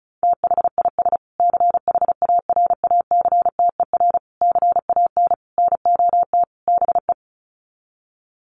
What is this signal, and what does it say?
Morse code, heard as a keyed tone, about 34 wpm, 700 Hz. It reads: THIS CHARACTER CAN NOT BE
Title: Sets of 5 Words Encore2 35wpm